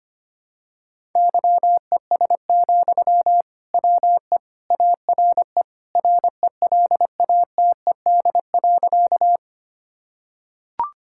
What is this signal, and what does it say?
Morse code, heard as a keyed tone, about 25 wpm, 700 Hz. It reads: YES, WE ARE RELATED.
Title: Sentences from Top 1000 Words - Review Part 5 - 25wpm